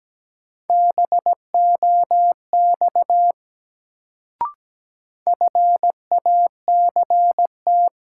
Morse code, heard as a keyed tone, about 17 wpm, 700 Hz. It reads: BOX FACT
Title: Sentences from Top 500 Words - New Word Drill - 17wpm